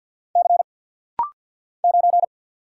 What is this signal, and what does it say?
Morse code, heard as a keyed tone, about 50 wpm, 700 Hz. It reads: C 9
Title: Single Letter-Number Alt - Rapid-Fire 50wpm